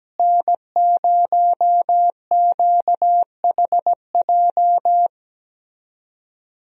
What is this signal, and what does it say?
Morse code, heard as a keyed tone, about 17 wpm, 700 Hz. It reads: N0QHJ
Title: USA Call Signs - 5 Characters - Repeated 2x - 17wpm